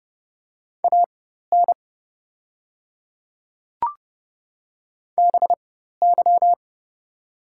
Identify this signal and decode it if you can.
Morse code, heard as a keyed tone, about 30 wpm, spaced as for 14 wpm, 700 Hz. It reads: AN BY
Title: 4x Character Spacing - Top 300 Words - 30wpm